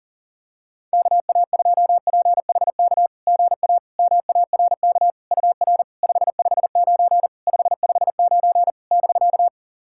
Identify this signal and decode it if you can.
Morse code, heard as a keyed tone, about 40 wpm, 690 Hz. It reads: KA2PHK GA MARK UR 559 559 <BK>
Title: SOTA QSO Practice 40wpm